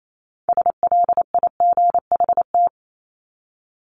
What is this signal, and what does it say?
Morse code, heard as a keyed tone, about 28 wpm, 700 Hz. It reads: SLIGHT
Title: LY Suffix - Top 100 Words with Base Words 28wpm